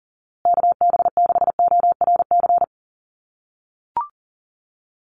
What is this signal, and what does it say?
Morse code, heard as a keyed tone, about 40 wpm, 700 Hz. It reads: KB6ORC
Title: USA Call Signs Repeated 2x 40wpm